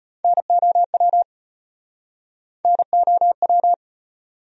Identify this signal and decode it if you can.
Morse code, heard as a keyed tone, about 38 wpm, 700 Hz. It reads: NOW NOW